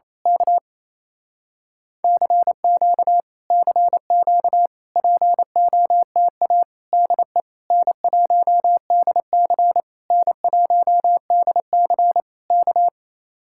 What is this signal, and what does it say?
Morse code, heard as a keyed tone, about 27 wpm, 700 Hz. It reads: K CQ CQ POTA DE N1DC N1DC K
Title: Speed-Racing Common QSO Phrases Encore - 28wpm